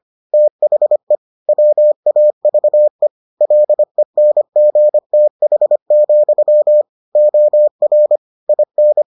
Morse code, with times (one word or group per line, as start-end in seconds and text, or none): 0.3-1.1 THE
1.5-3.1 WAVE
3.4-6.8 LENGTH,
7.2-8.2 OR
8.5-9.0 IN